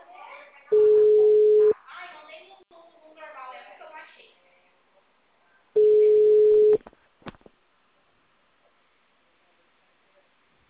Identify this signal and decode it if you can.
Ringback tone (Europe, 425 Hz), with speech.
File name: out- -2021- - - .4365